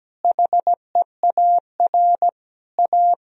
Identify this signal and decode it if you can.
Morse code, heard as a keyed tone, about 17 wpm, 700 Hz. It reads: HEAR A
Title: Sentences from Top 1000 Words - Review Part 1 - 17wpm